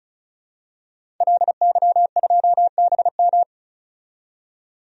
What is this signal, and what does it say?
Morse code, heard as a keyed tone, about 36 wpm, 700 Hz. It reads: LY2BM
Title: DXCC-100 Call Signs - 5 Characters - 35wpm